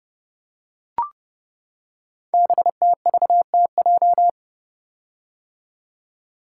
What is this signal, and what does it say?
Morse code, heard as a keyed tone, about 30 wpm, 700 Hz. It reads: BTVTJ
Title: Code Groups - 5 Letters - 30wpm